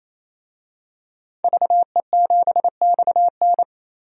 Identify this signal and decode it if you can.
Morse code, heard as a keyed tone, about 28 wpm, 700 Hz. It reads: VE7XN
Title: DXCC-100 Call Signs - 5 Characters - 28wpm